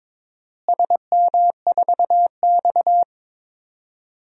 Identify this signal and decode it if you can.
Morse code, heard as a keyed tone, about 22 wpm, 700 Hz. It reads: SM4X
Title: DXCC-100 Call Signs - 4 Characters - Repeated 3x - 22wpm